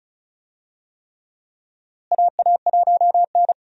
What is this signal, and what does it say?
Morse code, heard as a keyed tone, about 34 wpm, 700 Hz. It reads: AA1N
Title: USA Call Signs - 4 Characters - Repeated 3x - 35wpm